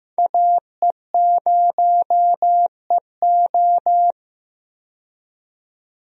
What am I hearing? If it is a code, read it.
Morse code, heard as a keyed tone, about 15 wpm, 700 Hz. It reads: AE0EO
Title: USA Call Signs - 5 Characters - 15wpm